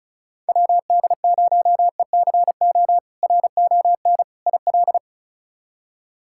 Morse code, heard as a keyed tone, about 36 wpm, 700 Hz. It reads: WD0ECO RON IL
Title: K1USN SST QSO Practice - 35wpm